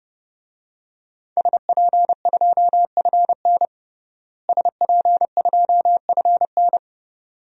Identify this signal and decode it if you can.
Morse code, heard as a keyed tone, about 30 wpm, 700 Hz. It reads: SP2FN SP2FN
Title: DXCC-100 Call Signs - 5 Characters - Repeated 2x - 30wpm